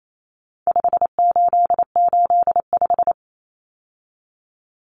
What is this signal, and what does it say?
Morse code, heard as a keyed tone, about 28 wpm, 700 Hz. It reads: 5885
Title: Numbers - 4-Digit 28wpm